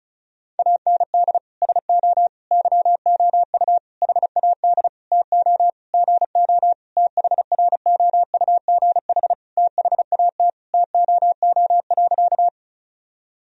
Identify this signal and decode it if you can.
Morse code, heard as a keyed tone, about 34 wpm, 700 Hz. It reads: AND SO YOU HAD TO GO THROUGH THAT TOO.
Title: Sentences from Top 1000 Words - Review Part 1 - 35wpm